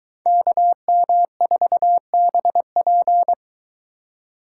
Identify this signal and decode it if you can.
Morse code, heard as a keyed tone, about 22 wpm, 700 Hz. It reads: KM4BP